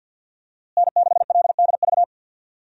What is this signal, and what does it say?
Morse code, heard as a keyed tone, about 47 wpm, 700 Hz. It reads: N6LDV
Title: USA Call Signs - 5 Characters - Repeated 2x - 50wpm